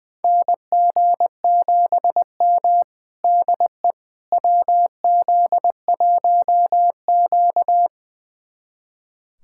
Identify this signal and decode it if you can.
Morse code, heard as a keyed tone, about 20 wpm, 700 Hz. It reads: NG7M DE WZ1Q
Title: USA Callsign DE Callsign - 4 Characters - 20wpm